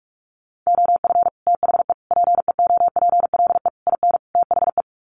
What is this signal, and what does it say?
Morse code, heard as a keyed tone, about 44 wpm, 700 Hz. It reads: OF THE PEOPLE IN THE
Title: Sets of 5 Words Encore 45wpm